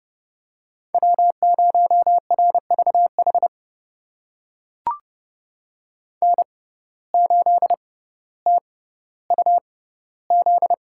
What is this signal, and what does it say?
Morse code, heard as a keyed tone, about 30 wpm, spaced as for 15 wpm, 690 Hz. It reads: W0RVH N8TUZ
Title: USA Call Signs - 5 Characters - 6x Character Spacing - 30wpm